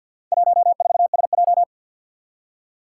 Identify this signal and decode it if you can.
Morse code, heard as a keyed tone, about 52 wpm, 700 Hz. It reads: 14SJ
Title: Four Challenging Letter-Number Combinations 50wpm